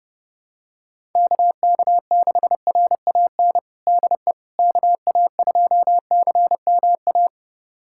Morse code, heard as a keyed tone, about 30 wpm, 700 Hz. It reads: KK6RAN DE KA2CMA
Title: USA Callsign DE Callsign - 6 Characters - 30wpm